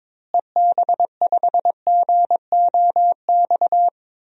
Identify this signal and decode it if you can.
Morse code, heard as a keyed tone, about 21 wpm, 700 Hz. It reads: EB5GOX
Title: DXCC-100 Call Signs - 6 Characters - Repeated 3x - 22wpm